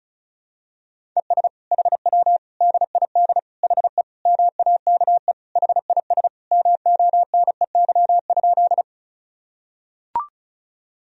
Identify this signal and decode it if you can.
Morse code, heard as a keyed tone, about 36 wpm, 700 Hz. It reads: ES HW DID HE MAKE HIS MONEY?
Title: Sentences from Top 400 Words and Abbreviations 35wpm